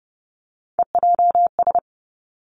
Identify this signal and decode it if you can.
Morse code, heard as a keyed tone, about 30 wpm, 700 Hz. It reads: EJS
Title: Three Challenging Letter-Number Combinations 30wpm